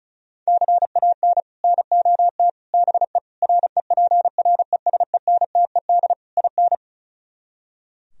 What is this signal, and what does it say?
Morse code, heard as a keyed tone, about 46 wpm, 700 Hz. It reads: CAN NOT BE REPRESENTED IN